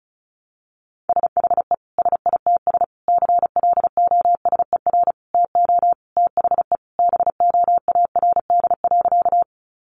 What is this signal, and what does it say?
Morse code, heard as a keyed tone, about 34 wpm, 700 Hz. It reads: SHE SITS CLOSER TO THE BOARD.
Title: Additional Easy Phrases 35wpm